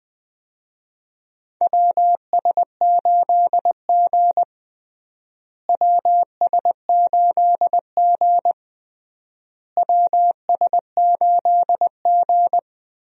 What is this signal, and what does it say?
Morse code, heard as a keyed tone, about 21 wpm, 700 Hz. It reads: WS8G WS8G WS8G
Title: USA Call Signs - 4 Characters - Repeated 3x - 20wpm